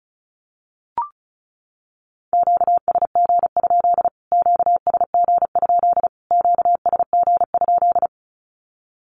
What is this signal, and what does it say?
Morse code, heard as a keyed tone, about 34 wpm, 700 Hz. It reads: QSG? QSG? QSG?
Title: Q-Codes Repeated 3x 35wpm